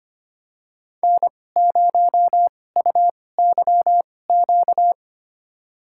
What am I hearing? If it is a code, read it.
Morse code, heard as a keyed tone, about 25 wpm, spaced as for 18 wpm, 700 Hz. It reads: N0UYQ
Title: USA Call Signs - 2x Character Spacing - 25wpm